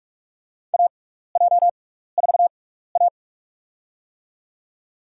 Morse code, heard as a keyed tone, about 46 wpm, spaced as for 15 wpm, 700 Hz. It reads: AJ4A
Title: USA Call Signs - 6x Character Spacing - 45wpm